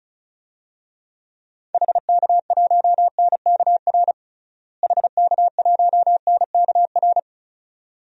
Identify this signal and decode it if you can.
Morse code, heard as a keyed tone, about 34 wpm, 700 Hz. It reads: HK1NKR HK1NKR
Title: DXCC-100 Call Signs - 6 Characters - Repeated 2x - 35wpm